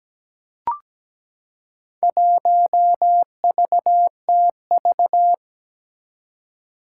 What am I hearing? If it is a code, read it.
Morse code, heard as a keyed tone, about 17 wpm, 700 Hz. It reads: 1VTV
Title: Four Challenging Letter-Number Combinations 17wpm